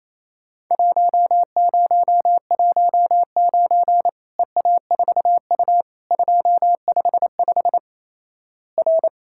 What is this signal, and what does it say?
Morse code, heard as a keyed tone, about 28 wpm, 700 Hz. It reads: 1019 EA4U 255 R
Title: Worked All Europe DX Contest - QTC Exchange - 28wpm